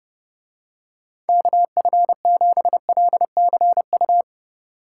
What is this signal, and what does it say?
Morse code, heard as a keyed tone, about 30 wpm, 700 Hz. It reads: KF7LCU